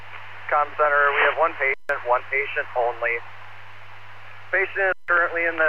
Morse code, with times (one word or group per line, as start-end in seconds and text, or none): none